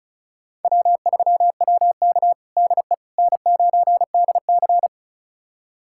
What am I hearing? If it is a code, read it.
Morse code, heard as a keyed tone, about 36 wpm, 700 Hz. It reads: W3WK DE N9DC
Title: USA Callsign DE Callsign - 4 Characters - 35wpm